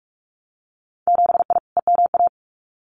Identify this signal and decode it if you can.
Morse code, heard as a keyed tone, about 44 wpm, 700 Hz. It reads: 7I EMA
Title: Winter Field Day Exchanges Encore 45wpm